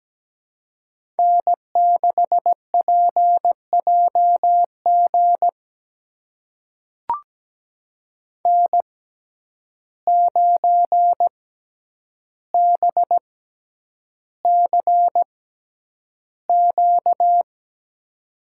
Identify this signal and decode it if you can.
Morse code, heard as a keyed tone, about 17 wpm, spaced as for 9 wpm, 700 Hz. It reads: N6PJG N9BCQ
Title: USA Call Signs - 5 Characters - 6x Character Spacing - 17wpm